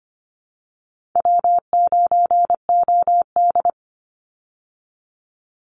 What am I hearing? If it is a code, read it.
Morse code, heard as a keyed tone, about 25 wpm, 700 Hz. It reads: W9OD
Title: Easy Call Signs Repeated 3x - 25wpm